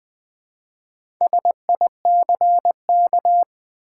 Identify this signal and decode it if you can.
Morse code, heard as a keyed tone, about 20 wpm, 700 Hz. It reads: SICK
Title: Build A Word - 1 Letter At A Time - 4-Letter Words - 20wpm